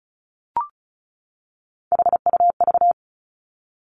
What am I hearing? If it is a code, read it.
Morse code, heard as a keyed tone, about 36 wpm, 700 Hz. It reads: HUV
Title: Three Challenging Letter-Number Combinations 35wpm